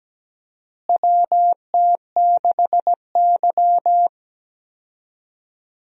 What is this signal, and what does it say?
Morse code, heard as a keyed tone, about 17 wpm, 700 Hz. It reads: WT6Y
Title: USA Call Signs - 4 Characters - Repeated 2x - 17wpm